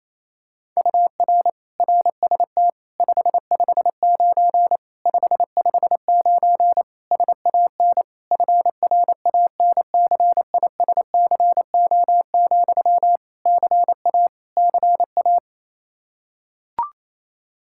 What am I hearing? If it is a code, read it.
Morse code, heard as a keyed tone, about 28 wpm, 700 Hz. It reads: UR RST 559 559 SAN FRANCISCO, CA CA
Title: Practice QSO - Exchange 1 - Beginning - 28wpm